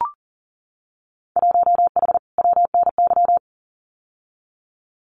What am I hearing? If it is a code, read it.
Morse code, heard as a keyed tone, about 41 wpm, 700 Hz. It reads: 1H WNY
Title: Winter Field Day Exchanges 40wpm